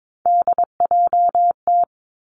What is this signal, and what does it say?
Morse code, heard as a keyed tone, about 22 wpm, 700 Hz. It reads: DJT
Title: Three Challenging Letter-Number Combinations 22wpm